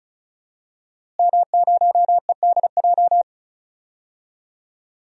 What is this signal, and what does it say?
Morse code, heard as a keyed tone, about 36 wpm, 700 Hz. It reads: M0EDJ
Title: DXCC-100 Call Signs - 5 Characters - Repeated 3x - 35wpm